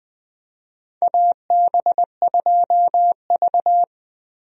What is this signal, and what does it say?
Morse code, heard as a keyed tone, about 20 wpm, 700 Hz. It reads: AB2V
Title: USA Call Signs - 4 Characters - Repeated 2x - 20wpm